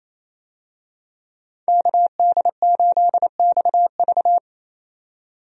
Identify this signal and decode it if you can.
Morse code, heard as a keyed tone, about 28 wpm, 700 Hz. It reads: KD8XV
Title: USA Call Signs - 5 Characters - 28wpm